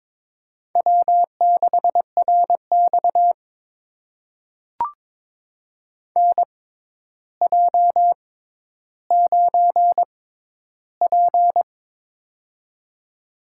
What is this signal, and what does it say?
Morse code, heard as a keyed tone, about 22 wpm, spaced as for 11 wpm, 700 Hz. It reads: W6RX NJ9P
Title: USA Call Signs - 4 Characters - 6x Character Spacing - 22wpm